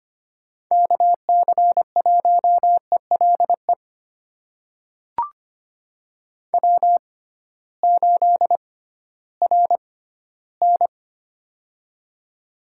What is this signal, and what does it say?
Morse code, heard as a keyed tone, about 25 wpm, spaced as for 15 wpm, 700 Hz. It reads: KC1ELE W8RN
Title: USA Call Signs - 6x Character Spacing - 25wpm